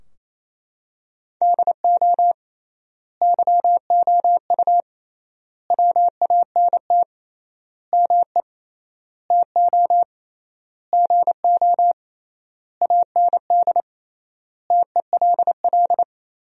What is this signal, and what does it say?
Morse code, heard as a keyed tone, about 28 wpm, 700 Hz. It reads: DO YOU WANT ME TO GO AND TELL
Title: 3x Word-Spacing - Sentences from Top 200 Words - 28wpm